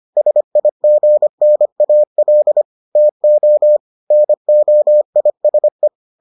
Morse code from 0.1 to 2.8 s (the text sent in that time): SIGNAL